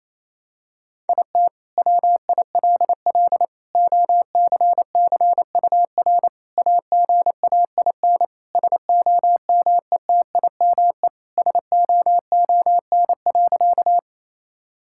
Morse code, heard as a keyed tone, about 28 wpm, 700 Hz. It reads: IT WILL OCCUR AGAIN SOMETIME SOON.
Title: Sentences from Top 1000 Words - Review Part 6 - 28wpm